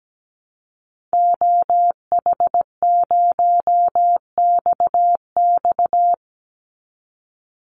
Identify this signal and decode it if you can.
Morse code, heard as a keyed tone, about 17 wpm, 700 Hz. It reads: OH0XX
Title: Call Signs Repeated 3x - 17wpm